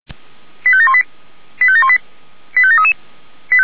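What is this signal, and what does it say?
Five-tone selective call (ZVEI), heard as quick runs of five tones, tones of about 80 ms. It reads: 85718 85718 85733
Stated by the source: ZVEI five-tone selective call received by SDR, from feuerwehr-orlamuende.org fire brigade archive.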